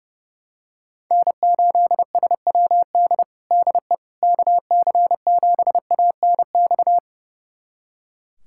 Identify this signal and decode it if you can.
Morse code, heard as a keyed tone, about 30 wpm, 700 Hz. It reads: N8SWD DE KC7ANX
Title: USA Callsign DE Callsign 30wpm